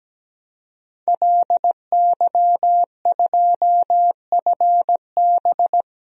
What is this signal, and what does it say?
Morse code, heard as a keyed tone, about 17 wpm, 700 Hz. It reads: LY2FB
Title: DXCC-100 Call Signs - 5 Characters - Repeated 2x - 17wpm